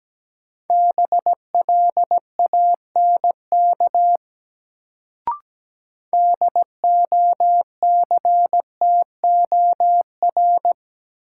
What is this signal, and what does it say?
Morse code, heard as a keyed tone, about 17 wpm, 700 Hz. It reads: BLANK DOCTOR
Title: Sentences from Top 1500 Words - New Word Drill Alt - 17wpm